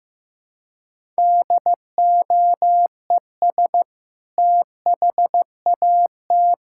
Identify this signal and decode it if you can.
Morse code, heard as a keyed tone, about 15 wpm, 700 Hz. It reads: DOES THAT
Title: Sentences from Top 1600 Words 15wpm